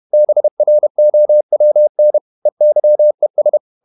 Morse code, heard as a keyed tone, about 31 wpm, 600 Hz. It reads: BROWN EYES
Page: Never 31wpm | CW med Gnister